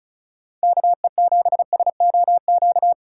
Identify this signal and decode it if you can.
Morse code, heard as a keyed tone, about 34 wpm, 700 Hz. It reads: KE7SOQ
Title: USA Callsign DE Callsign 35wpm